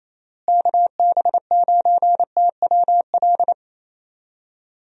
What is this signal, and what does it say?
Morse code, heard as a keyed tone, about 28 wpm, 700 Hz. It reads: KB9TWL
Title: USA Call Signs - 6 Characters - Repeated 2x - 28wpm